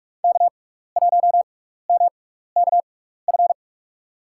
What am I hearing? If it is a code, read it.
Morse code, heard as a keyed tone, about 42 wpm, spaced as for 15 wpm, 690 Hz. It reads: K1MKF
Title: USA Call Signs - 5 Characters - 6x Character Spacing - 45wpm